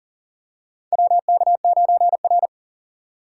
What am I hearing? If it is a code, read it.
Morse code, heard as a keyed tone, about 40 wpm, 700 Hz. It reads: WK9R
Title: USA Call Signs - 4 Characters - 3x Character Spacing - 40wpm